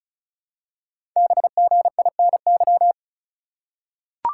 Morse code, heard as a keyed tone, about 35 wpm, 700 Hz. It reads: BGINY
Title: Code Groups - 5 Letters - 35wpm